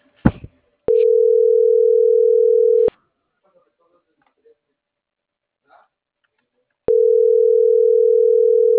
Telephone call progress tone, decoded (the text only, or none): ringback tone